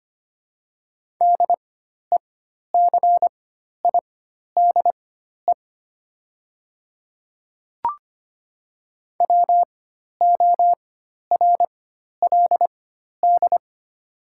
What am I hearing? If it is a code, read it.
Morse code, heard as a keyed tone, about 25 wpm, spaced as for 12 wpm, 690 Hz. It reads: DECIDE WORLD
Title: 4x Character Spacing - Top 500 Words - 25wpm